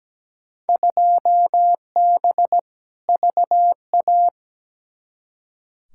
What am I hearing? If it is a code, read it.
Morse code, heard as a keyed tone, about 17 wpm, 700 Hz. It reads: 2B VA
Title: Field Day Exchanges Encore - 17wpm